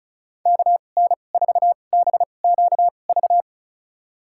Morse code, heard as a keyed tone, about 36 wpm, spaced as for 25 wpm, 700 Hz. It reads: KN4BQV
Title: USA Call Signs - 2x Character Spacing - 35wpm